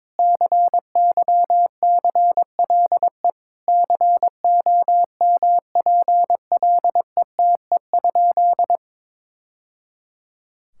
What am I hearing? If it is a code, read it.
Morse code, heard as a keyed tone, about 22 wpm, 700 Hz. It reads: CYCLE COMPLETE?
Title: Sentences from Top 1000 Words - Review Part 6 - 22wpm